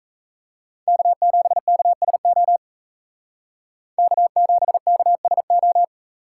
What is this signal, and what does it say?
Morse code, heard as a keyed tone, about 34 wpm, 700 Hz. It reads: K7KSO K7KSO
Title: Speed Racing - USA 1x3 Call Signs 28wpm